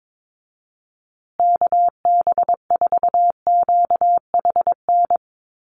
Morse code, heard as a keyed tone, about 22 wpm, 700 Hz. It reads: KB4QHN
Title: USA Call Signs Repeated 3x 22wpm